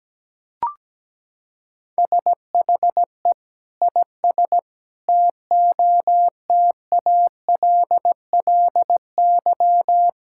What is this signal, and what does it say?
Morse code, heard as a keyed tone, about 17 wpm, 700 Hz. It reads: SHE IS TOTALLY
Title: Sentences from Top 1600 Words 17wpm